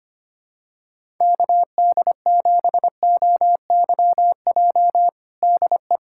Morse code, heard as a keyed tone, about 25 wpm, 700 Hz. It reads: KD7OYJ DE
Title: USA Callsign DE Callsign - 6 Characters - 25wpm